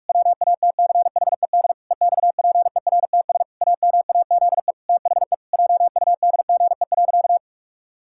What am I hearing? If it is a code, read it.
Morse code, heard as a keyed tone, about 41 wpm, 700 Hz. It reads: WATCHED EXPERTS AMAZE THE JUDGE.
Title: Pangram Challenge v2 - 45wpm